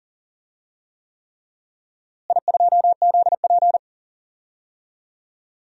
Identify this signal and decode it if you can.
Morse code, heard as a keyed tone, about 40 wpm, 700 Hz. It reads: I2ZP
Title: DXCC-100 Call Signs - 4 Characters - Repeated 3x - 40wpm